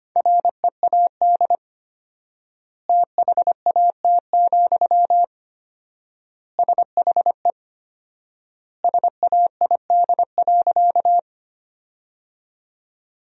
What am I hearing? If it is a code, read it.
Morse code, heard as a keyed tone, about 25 wpm, 700 Hz. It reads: READ THAT, SHE SAID.
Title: 4x Word-Spacing - Sentences from Top 200 Words - 25wpm